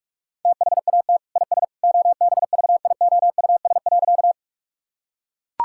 Morse code, heard as a keyed tone, about 41 wpm, 700 Hz. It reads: THAT IS OBVIOUS.
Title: Sentences from Top 1600 Words 45wpm